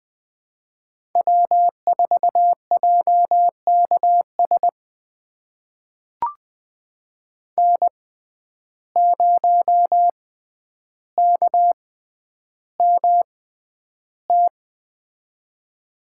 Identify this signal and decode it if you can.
Morse code, heard as a keyed tone, about 20 wpm, spaced as for 10 wpm, 700 Hz. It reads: W4JKS N0KMT
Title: USA Call Signs - 5 Characters - 6x Character Spacing - 20wpm